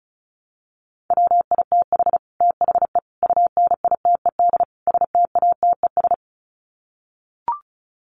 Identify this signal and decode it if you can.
Morse code, heard as a keyed tone, about 34 wpm, 700 Hz. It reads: WITH THE UNITED STATES
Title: Sets of 4 Words 35wpm